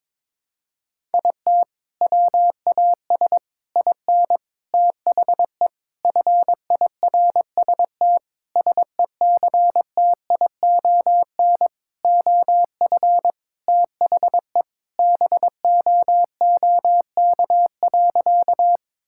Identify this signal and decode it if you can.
Morse code, heard as a keyed tone, about 21 wpm, 700 Hz. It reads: IT WAS IN THE FIRST SECTION OF THE BOOK.
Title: Sentences from Top 1000 Words - Review Part 5 - 22wpm